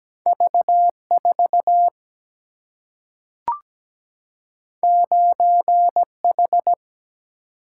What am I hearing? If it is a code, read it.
Morse code, heard as a keyed tone, about 17 wpm, 700 Hz. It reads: V4 9H
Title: Single-Letter Single-Number 17wpm